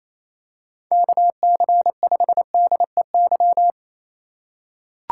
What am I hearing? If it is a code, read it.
Morse code, heard as a keyed tone, about 28 wpm, 700 Hz. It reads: KC5DEY
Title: USA Call Signs - 6 Characters - 4x Character Spacing - 28wpm